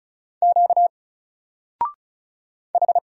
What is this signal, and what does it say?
Morse code, heard as a keyed tone, about 36 wpm, 700 Hz. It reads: Q H
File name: Single-Letter-Number-35wpm